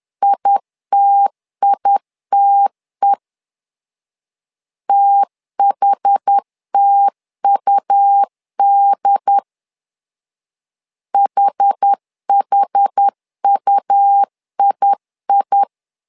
Morse code, heard as a keyed tone, about 10 wpm, 800 Hz. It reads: ITITE THTUD HHUII
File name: LEZIONI / 2 UHDN / 2 UHD 10wpm 1826